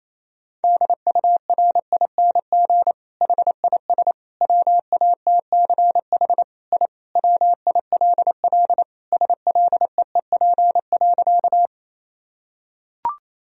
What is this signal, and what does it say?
Morse code, heard as a keyed tone, about 28 wpm, 700 Hz. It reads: DURING HIS WATCH I WILL SLEEP.
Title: Sentences from Top 1000 Words - Review Part 2 - 28wpm